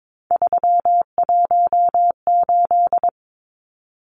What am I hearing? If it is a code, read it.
Morse code, heard as a keyed tone, about 22 wpm, 700 Hz. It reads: 318
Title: Numbers - 3-Digit 22wpm